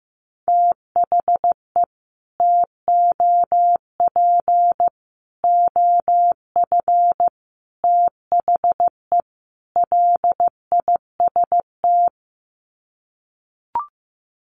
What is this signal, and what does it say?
Morse code, heard as a keyed tone, about 15 wpm, 700 Hz. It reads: THE TOP OF THE LIST
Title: Sets of 5 Words Encore2 15wpm